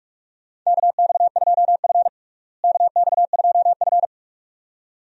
Morse code, heard as a keyed tone, about 41 wpm, 700 Hz. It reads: KX2F KX2F
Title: USA Call Signs - 4 Characters - Repeated 2x - 45wpm